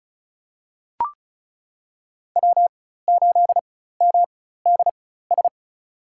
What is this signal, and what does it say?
Morse code, heard as a keyed tone, about 34 wpm, spaced as for 16 wpm, 700 Hz. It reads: W8MDS
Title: USA Call Signs - 5 Characters - 4x Character Spacing - 35wpm